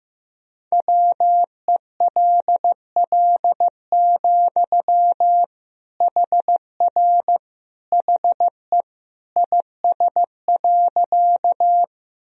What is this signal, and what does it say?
Morse code, heard as a keyed tone, about 15 wpm, 700 Hz. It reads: WELL, HR HE IS.
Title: Sentences from Top 200 Words and Abbreviations 15wpm